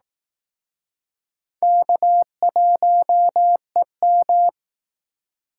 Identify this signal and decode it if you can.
Morse code, heard as a keyed tone, about 19 wpm, 700 Hz. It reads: K1EM